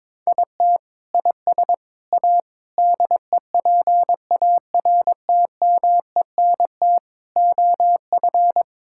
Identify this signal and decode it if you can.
Morse code, heard as a keyed tone, about 22 wpm, 700 Hz. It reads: IT IS A DEPARTMENT OF
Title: Sentences from Top 1000 Words - Review Part 5 - 22wpm